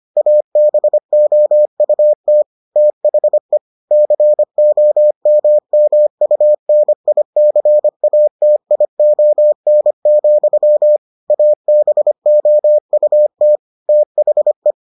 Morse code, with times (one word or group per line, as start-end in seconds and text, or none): 0.2-2.4 ABOUT
2.8-3.6 THE
3.9-11.0 COMMUNICATION,
11.3-13.6 ABOUT
13.9-14.7 THE